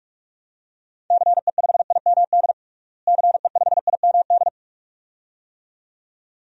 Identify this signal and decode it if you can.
Morse code, heard as a keyed tone, about 46 wpm, 700 Hz. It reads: CE5IMD CE5IMD
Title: DXCC-100 Call Signs - 6 Characters - Repeated 2x - 45wpm